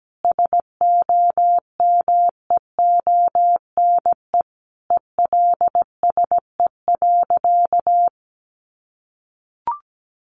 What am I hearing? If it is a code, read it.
Morse code, heard as a keyed tone, about 17 wpm, 700 Hz. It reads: SOMEONE ELSE.
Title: Sentences from Top 800 Words 17wpm